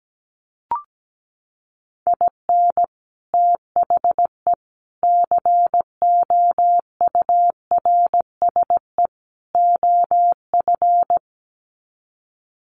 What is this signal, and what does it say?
Morse code, heard as a keyed tone, about 17 wpm, 700 Hz. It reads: IN THE COURSE OF
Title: Sets of 4 Words 17wpm